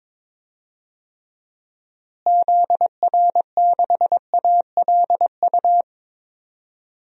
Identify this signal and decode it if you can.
Morse code, heard as a keyed tone, about 22 wpm, 700 Hz. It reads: ZR6ALU
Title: DXCC-100 Call Signs - 6 Characters - Repeated 3x - 22wpm